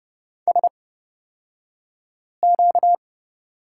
Morse code, heard as a keyed tone, about 30 wpm, spaced as for 5 wpm, 700 Hz. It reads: SQ
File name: Lesson 38d - Letter q - Two Characters 30wpm